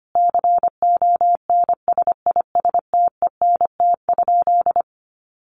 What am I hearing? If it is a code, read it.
Morse code, heard as a keyed tone, about 25 wpm, 700 Hz. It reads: CONSISTENT?
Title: Sentences from Top 1500 Words 25wpm